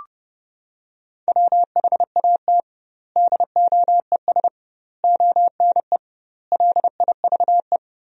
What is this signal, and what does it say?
Morse code, heard as a keyed tone, about 30 wpm, 700 Hz. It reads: WHAT DOES ONE LIVE
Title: 2x Word-Spacing - Sentences from Top 200 Words - 30wpm